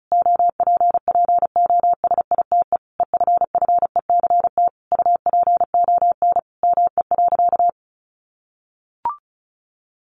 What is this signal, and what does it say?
Morse code, heard as a keyed tone, about 34 wpm, 700 Hz. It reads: OPPOSITE EFFECT UPON ME.
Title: Sentences from Top 1300 Words 35wpm